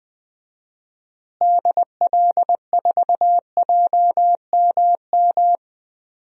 Morse code, heard as a keyed tone, about 20 wpm, 700 Hz. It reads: DL4JMM
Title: DXCC-100 Call Signs - 6 Characters - Repeated 2x - 20wpm